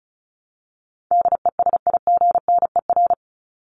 Morse code, heard as a keyed tone, about 34 wpm, 700 Hz. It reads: DESIGNER
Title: Top 1501-1750 Words 35wpm